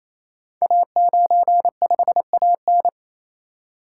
Morse code, heard as a keyed tone, about 27 wpm, 700 Hz. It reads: A95AN
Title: Code Groups - 5 Letter:Numbers - 28wpm